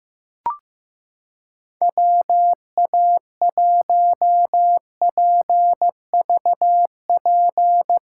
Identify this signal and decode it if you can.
Morse code, heard as a keyed tone, about 15 wpm, 700 Hz. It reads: WA1PVP
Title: USA Callsign DE Callsign - 6 Characters - 15wpm